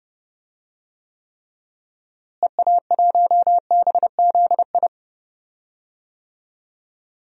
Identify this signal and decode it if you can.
Morse code, heard as a keyed tone, about 30 wpm, 700 Hz. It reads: EA1BZI
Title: DXCC-100 Call Signs - 6 Characters - Repeated 3x - 30wpm